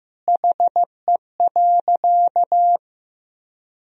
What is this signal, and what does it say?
Morse code, heard as a keyed tone, about 15 wpm, 700 Hz. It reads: HE.
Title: Sentences from Top 100 Words and Abbreviations 15wpm